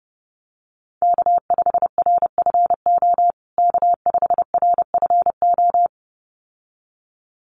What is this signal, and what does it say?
Morse code, heard as a keyed tone, about 30 wpm, 700 Hz. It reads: K5RFO K5RFO
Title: USA Call Signs Repeated 2x 30wpm